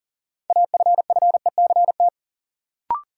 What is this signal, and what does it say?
Morse code, heard as a keyed tone, about 40 wpm, 700 Hz. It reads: AFFECT
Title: Sentences from Top 900 Words - New Word Drill - 40wpm